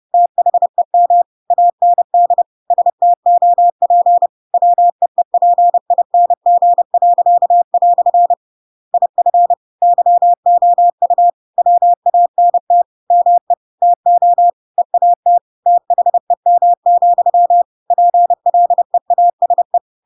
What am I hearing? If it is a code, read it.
Morse code, heard as a keyed tone, about 30 wpm, 700 Hz. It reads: THEM AND STOP WEEPING." IF YOU WANT ME TO EAT THEM, PLEASE
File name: Pinocchio / Pinocchio ch7 30 30 ik1hge